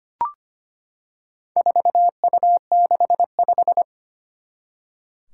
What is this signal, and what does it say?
Morse code, heard as a keyed tone, about 25 wpm, 700 Hz. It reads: 4U65
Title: Four Challenging Letter-Number Combinations 25wpm